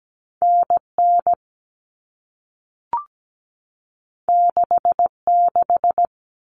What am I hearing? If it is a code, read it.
Morse code, heard as a keyed tone, about 17 wpm, 700 Hz. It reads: NN 66
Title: Common Cut Numbers - 2-Digit 17wpm